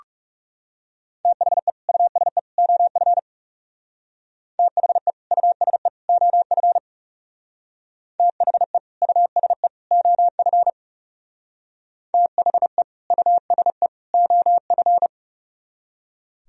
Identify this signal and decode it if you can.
Morse code, heard as a keyed tone, about 39 wpm, 700 Hz. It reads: THE USE OF THE USE OF THE USE OF THE USE OF